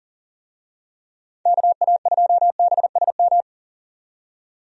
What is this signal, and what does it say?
Morse code, heard as a keyed tone, about 39 wpm, 700 Hz. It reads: KA2BSM
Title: USA Call Signs - 6 Characters - 40wpm